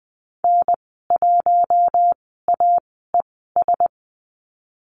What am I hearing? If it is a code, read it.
Morse code, heard as a keyed tone, about 20 wpm, spaced as for 15 wpm, 700 Hz. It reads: N1AES
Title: 042 Lesson 8g - Number 1 - Call Signs 20wpm